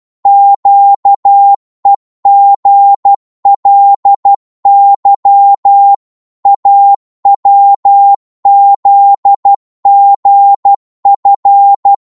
Morse code, heard as a keyed tone, about 12 wpm, 800 Hz. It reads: QEGLY AWZGF
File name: Morse / ltr-12wpm